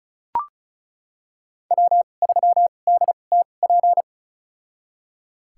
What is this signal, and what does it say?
Morse code, heard as a keyed tone, about 36 wpm, spaced as for 25 wpm, 700 Hz. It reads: W3DTP
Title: USA Call Signs - 5 Characters - 2x Character Spacing - 35wpm